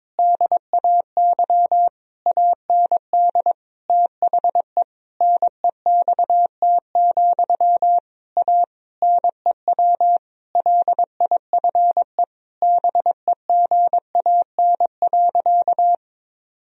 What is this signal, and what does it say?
Morse code, heard as a keyed tone, about 22 wpm, 700 Hz. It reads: DAY AND THE NEXT, A NEW LIFE BEGAN.
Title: Sentences from Top 1000 Words - Review Part 1 - 22wpm